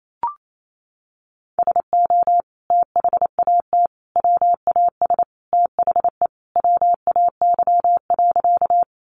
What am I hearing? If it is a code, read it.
Morse code, heard as a keyed tone, about 28 wpm, 700 Hz. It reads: SO THAT WAS THE WAY.
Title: Sentences from Top 100 Words 28wpm